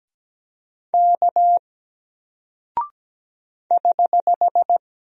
Morse code, heard as a keyed tone, about 17 wpm, 700 Hz. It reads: K <HH>
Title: Single Character (Letter:Number:Punctuation:Prosign) Alt 17wpm